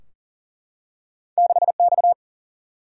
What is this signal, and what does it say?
Morse code, heard as a keyed tone, about 40 wpm, 700 Hz. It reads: 6X
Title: Single-Letter Single-Number 40wpm